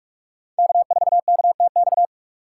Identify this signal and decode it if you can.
Morse code, heard as a keyed tone, about 46 wpm, 700 Hz. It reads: K4KTX
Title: USA Call Signs - 2x Character Spacing - 45wpm